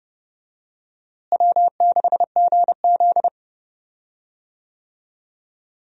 Morse code, heard as a keyed tone, about 31 wpm, 700 Hz. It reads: W6GZ
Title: USA Call Signs - 4 Characters - Repeated 2x - 30wpm